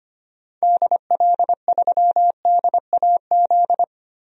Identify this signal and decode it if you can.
Morse code, heard as a keyed tone, about 25 wpm, 700 Hz. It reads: DL3DAZ
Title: DXCC-100 Call Signs - 6 Characters - 25wpm